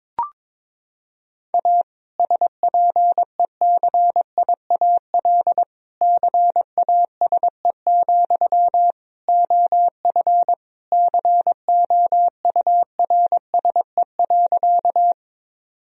Morse code, heard as a keyed tone, about 22 wpm, 700 Hz. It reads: A SPECIAL CASE, OF COURSE.
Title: Sentences from Top 1000 Words - Review Part 3 - 22wpm